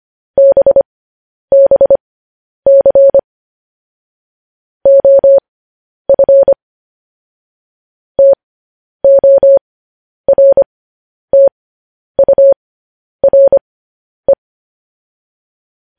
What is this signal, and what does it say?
Morse code, heard as a keyed tone, about 25 wpm, spaced as for 10 wpm, 550 Hz. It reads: BBC OF TORTURE
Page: news headlines in morse code at 10 wpm